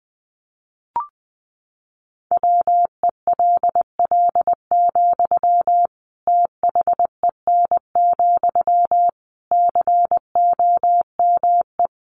Morse code, heard as a keyed tone, about 20 wpm, 700 Hz. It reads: WELL, THEN, COME
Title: Sentences from Top 700 Words 20wpm